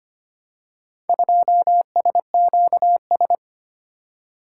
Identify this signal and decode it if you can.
Morse code, heard as a keyed tone, about 25 wpm, 700 Hz. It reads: 2SQS
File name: practice / 25wpm / Four Letter-Number Combinations - 25wpm